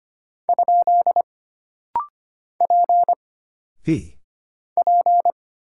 Morse code, heard as a keyed tone, about 25 wpm, 700 Hz. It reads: ? P P
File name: Single Character (Letter:Number:Punctuation:Prosign) - Rapid-Fire - 25wpm